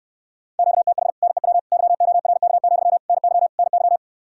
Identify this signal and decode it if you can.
Morse code, heard as a keyed tone, about 69 wpm, 700 Hz. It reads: QTH NEW YORK, NY NY